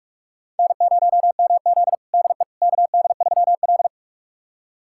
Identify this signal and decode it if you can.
Morse code, heard as a keyed tone, about 44 wpm, 690 Hz. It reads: N0MZ DE KD3L
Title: USA Callsign DE Callsign - 4 Characters - 45wpm